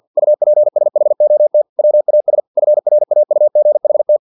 Morse code, heard as a keyed tone, about 50 wpm, 600 Hz. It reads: UPSHOT WAS FRAUGHT
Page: Never 49wpm